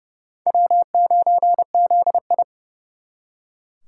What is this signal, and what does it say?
Morse code, heard as a keyed tone, about 38 wpm, 700 Hz. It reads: W9ZI